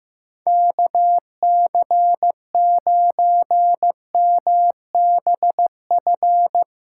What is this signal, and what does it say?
Morse code, heard as a keyed tone, about 16 wpm, 700 Hz. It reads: KC9MBF
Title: USA Callsign DE Callsign - 6 Characters - 15wpm